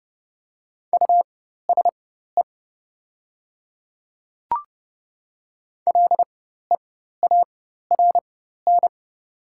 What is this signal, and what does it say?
Morse code, heard as a keyed tone, about 30 wpm, spaced as for 14 wpm, 700 Hz. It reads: USE LEARN
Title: 4x Character Spacing - Top 300 Words - 30wpm